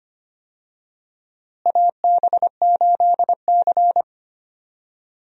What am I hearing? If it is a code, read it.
Morse code, heard as a keyed tone, about 25 wpm, 700 Hz. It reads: AB8C
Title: USA Call Signs - 4 Characters - Repeated 2x - 25wpm